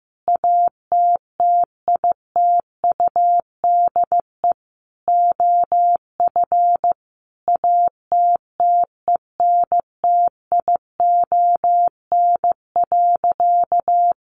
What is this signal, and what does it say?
Morse code, heard as a keyed tone, about 15 wpm, 700 Hz. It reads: ATTITUDE OF ATTENTION.
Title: Sentences from Top 1300 Words 15wpm